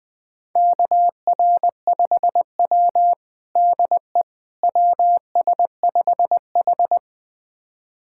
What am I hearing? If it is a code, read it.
Morse code, heard as a keyed tone, about 20 wpm, 700 Hz. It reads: KR5W DE WS5H
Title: USA Callsign DE Callsign - 4 Characters - 20wpm